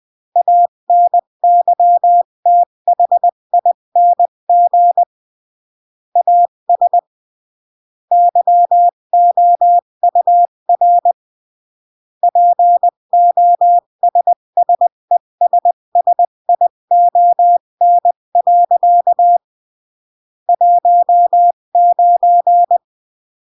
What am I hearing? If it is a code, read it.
Morse code, heard as a keyed tone, about 20 wpm, 700 Hz. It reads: ANYTHING AS YOUR POSSESSION. 19
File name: CW / jodorowsky / 82 commandments 20-15